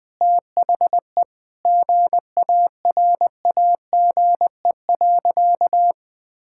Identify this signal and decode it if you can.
Morse code, heard as a keyed tone, about 20 wpm, 700 Hz. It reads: THE GARAGE.
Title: Sentences from Top 1600 Words 20wpm